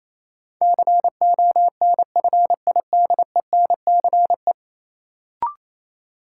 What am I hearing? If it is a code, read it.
Morse code, heard as a keyed tone, about 29 wpm, 700 Hz. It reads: CONFIDENCE
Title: Sentences from Top 1500 Words - New Word Drill - 28wpm